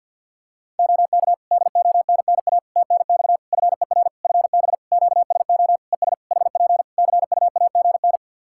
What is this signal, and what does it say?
Morse code, heard as a keyed tone, about 50 wpm, 700 Hz. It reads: OK DONNA TNX FER FB QSO ES HP CUAGN
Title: Practice QSO - Exchange 4 - Ending - 50wpm